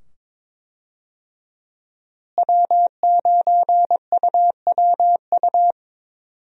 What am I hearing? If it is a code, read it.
Morse code, heard as a keyed tone, about 22 wpm, 700 Hz. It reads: W9UWU
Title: USA Call Signs - 5 Characters - Repeated 2x - 22wpm